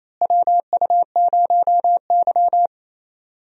Morse code, heard as a keyed tone, about 27 wpm, 700 Hz. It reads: WU0Y
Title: USA Call Signs - 4 Characters - Repeated 2x - 28wpm